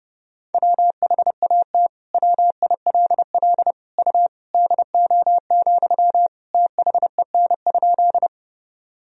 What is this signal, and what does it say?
Morse code, heard as a keyed tone, about 31 wpm, 700 Hz. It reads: WHAT WILL U DO, THEN?
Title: Sentences from Top 100 Words and Abbreviations 30wpm